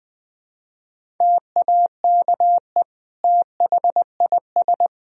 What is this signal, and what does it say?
Morse code, heard as a keyed tone, about 20 wpm, 700 Hz. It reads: TAKE THIS
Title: Sentences from Top 1000 Words - Review Part 4 - 20wpm